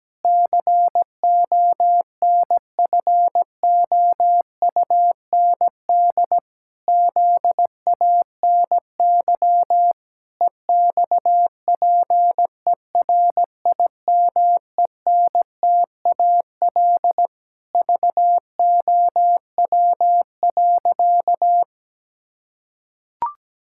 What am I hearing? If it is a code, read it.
Morse code, heard as a keyed tone, about 17 wpm, 700 Hz. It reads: CONFOUND ZANY EXPERIMENTAL VOW.
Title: Pangram Challenge v2 - 17wpm